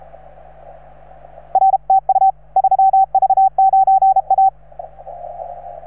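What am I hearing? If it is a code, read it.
Morse code, heard as a keyed tone, about 35 wpm, 770 Hz. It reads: R TU 3V9A
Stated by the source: Morse from station 3V9A.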